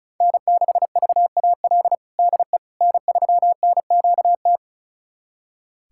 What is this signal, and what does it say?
Morse code, heard as a keyed tone, about 36 wpm, 700 Hz. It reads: N6VAL DE N3NQT
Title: USA Callsign DE Callsign - 5 Characters - 35wpm